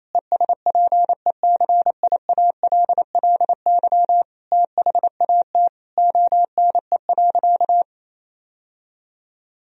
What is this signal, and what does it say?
Morse code, heard as a keyed tone, about 28 wpm, 700 Hz. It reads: ESPECIALLY THAT ONE.
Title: Sentences from Top 1000 Words - Review Part 4 - 28wpm